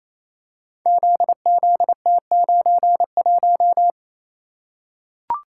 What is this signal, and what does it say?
Morse code, heard as a keyed tone, about 28 wpm, 700 Hz. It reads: ZZT91
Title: Code Groups - 5 Letter:Numbers - 28wpm